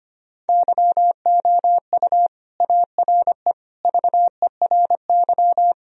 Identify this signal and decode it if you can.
Morse code, heard as a keyed tone, about 25 wpm, 700 Hz. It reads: YOU ARE VERY
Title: Sentences from Top 1000 Words - Review Part 1 - 25wpm